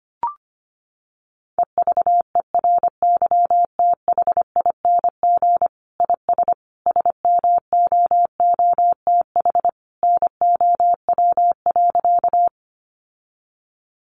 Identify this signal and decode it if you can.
Morse code, heard as a keyed tone, about 25 wpm, 700 Hz. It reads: EVERYTHING IS SMOOTH NOW.
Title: Sentences from Top 1400 Words 25wpm